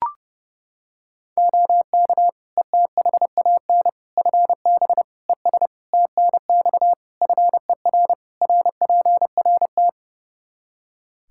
Morse code, heard as a keyed tone, about 30 wpm, 700 Hz. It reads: OK ETHAN FB ES TNX FER RPRT
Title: Practice QSO - Exchange 2 - Info - 30wpm